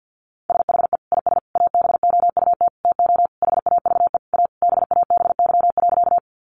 Morse code, heard as a keyed tone, about 50 wpm, 700 Hz. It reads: SHE IS ABOUT TO HAVE A BABY.
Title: Sentences from Top 1300 Words 50wpm